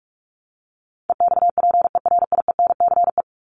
Morse code, heard as a keyed tone, about 44 wpm, 700 Hz. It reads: EXPERIENCE
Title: Top 501-750 Words 45wpm